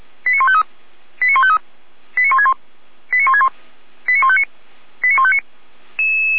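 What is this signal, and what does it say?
Five-tone selective call (ZVEI), heard as quick runs of five tones, tones of about 70 ms. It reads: 89153 89153 89161 89161 89159 89159
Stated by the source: ZVEI five-tone call received by SDR.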